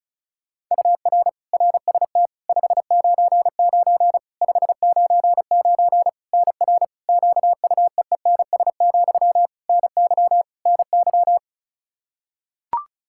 Morse code, heard as a keyed tone, about 34 wpm, 700 Hz. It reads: UR RST 599 599 NR QUEENS, NY NY
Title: K1USN SST QSO Practice 35wpm